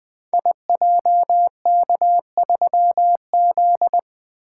Morse code, heard as a keyed tone, about 20 wpm, 700 Hz. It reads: IJK3Z
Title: Code Groups - 5 Letter:Numbers - 20wpm